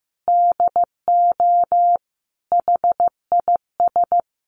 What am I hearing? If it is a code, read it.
Morse code, heard as a keyed tone, about 15 wpm, 700 Hz. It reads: DO HIS
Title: Sentences from Top 1500 Words 15wpm